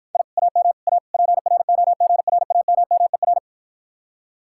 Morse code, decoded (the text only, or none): I AM A PROGRAMMER